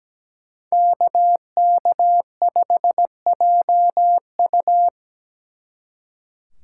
Morse code, heard as a keyed tone, about 19 wpm, 700 Hz. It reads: KK5JU
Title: Speed Racing - USA 2x2 Call Signs 17wpm